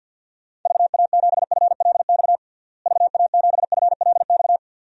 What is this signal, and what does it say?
Morse code, heard as a keyed tone, about 51 wpm, 700 Hz. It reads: VA7FLX VA7FLX
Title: DXCC-100 Call Signs - 6 Characters - Repeated 2x - 50wpm